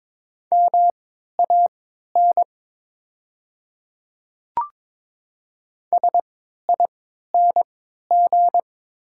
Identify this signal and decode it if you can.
Morse code, heard as a keyed tone, about 22 wpm, spaced as for 13 wpm, 690 Hz. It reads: MAN SING
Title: 3x Character Spacing - Top 500 Words - 22wpm